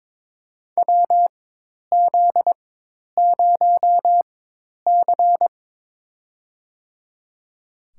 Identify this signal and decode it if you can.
Morse code, heard as a keyed tone, about 22 wpm, spaced as for 10 wpm, 700 Hz. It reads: WZ0C
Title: USA Call Signs - 4 Characters - 4x Character Spacing - 22wpm